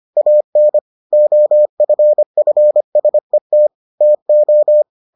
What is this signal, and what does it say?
Morse code, heard as a keyed tone, about 25 wpm, 600 Hz. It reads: AN OFFSET TO